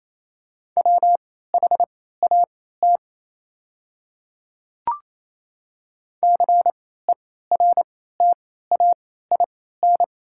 Morse code, heard as a keyed tone, about 28 wpm, spaced as for 16 wpm, 700 Hz. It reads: WHAT CERTAIN
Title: 3x Character Spacing - Top 500 Words - 28wpm